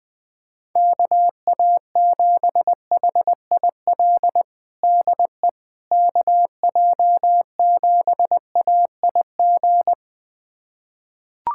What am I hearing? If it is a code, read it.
Morse code, heard as a keyed tone, about 20 wpm, 700 Hz. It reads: KA7HIL DE KJ7AIG
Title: USA Callsign DE Callsign - 6 Characters - 20wpm